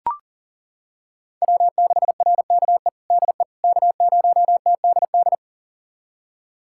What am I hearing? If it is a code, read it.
Morse code, heard as a keyed tone, about 37 wpm, 700 Hz. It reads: W6RKE DE K0TDD
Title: USA Callsign DE Callsign - 5 Characters - 40wpm